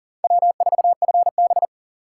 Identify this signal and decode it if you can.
Morse code, heard as a keyed tone, about 40 wpm, 700 Hz. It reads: W4FB
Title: USA Call Signs - 4 Characters - 4x Character Spacing - 40wpm